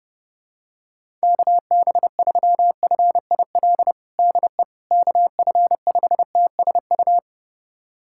Morse code, heard as a keyed tone, about 32 wpm, 690 Hz. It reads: KB3FIL DE KF5TSU
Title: USA Callsign DE Callsign - 6 Characters - 30wpm